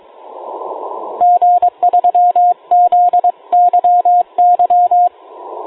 Morse code, heard as a keyed tone, about 22 wpm, 710 Hz, with radio noise. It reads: G3ZYY